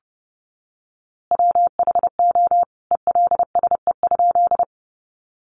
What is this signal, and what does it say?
Morse code, heard as a keyed tone, about 30 wpm, 700 Hz. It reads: WHO ELSE?
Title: A Sound Of Thunder 30wpm